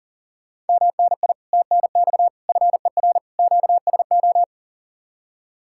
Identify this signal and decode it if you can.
Morse code, heard as a keyed tone, about 48 wpm, 700 Hz. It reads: MNI TNX FER QSO
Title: Speed-Racing Common QSO Phrases - 40wpm